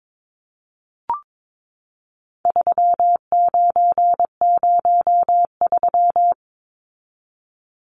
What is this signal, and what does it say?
Morse code, heard as a keyed tone, about 22 wpm, 700 Hz. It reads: 3903
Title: Numbers - 4-Digit 22wpm